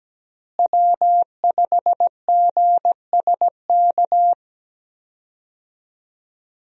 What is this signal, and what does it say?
Morse code, heard as a keyed tone, about 17 wpm, 700 Hz. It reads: W5GSK
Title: USA Call Signs - 5 Characters - Repeated 2x - 17wpm